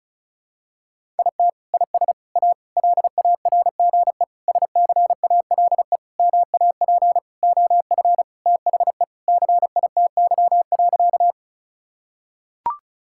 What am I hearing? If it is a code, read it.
Morse code, heard as a keyed tone, about 34 wpm, 700 Hz. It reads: IT IS A LARGE SCALE MAP OF THE CITY.
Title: Sentences from Top 1000 Words - Review Part 5 - 35wpm